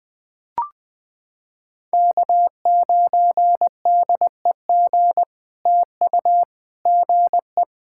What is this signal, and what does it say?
Morse code, heard as a keyed tone, about 21 wpm, 700 Hz. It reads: K9DEG TU GE
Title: POTA QSO Practice - 20wpm